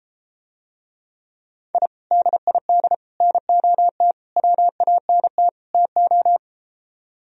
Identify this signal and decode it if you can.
Morse code, heard as a keyed tone, about 33 wpm, 700 Hz. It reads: I DID NOT WANT TO